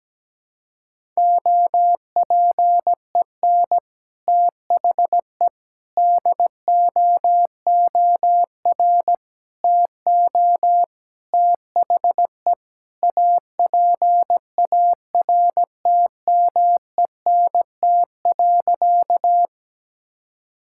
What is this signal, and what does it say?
Morse code, heard as a keyed tone, about 17 wpm, 700 Hz. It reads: OPEN THE DOOR TO THE APARTMENT.
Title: Sentences from Top 1000 Words - Review Part 6 - 17wpm